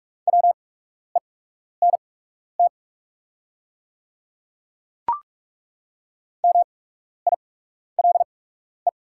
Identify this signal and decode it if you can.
Morse code, heard as a keyed tone, about 43 wpm, spaced as for 12 wpm, 690 Hz. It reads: WENT MILE
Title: 8x Character Spacing - Top 500 Words - 45wpm